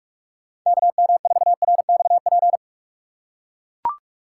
Morse code, heard as a keyed tone, about 44 wpm, 700 Hz. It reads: KM4RXP